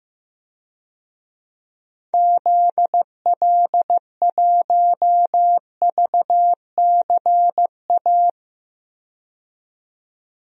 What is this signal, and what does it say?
Morse code, heard as a keyed tone, about 15 wpm, 700 Hz. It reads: ZL1VCA
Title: DXCC-100 Call Signs - 6 Characters - 15wpm